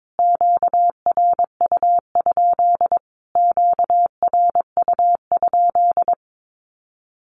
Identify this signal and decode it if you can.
Morse code, heard as a keyed tone, about 22 wpm, 700 Hz. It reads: QRU? QRU?
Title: Q-Codes Repeated 3x 22wpm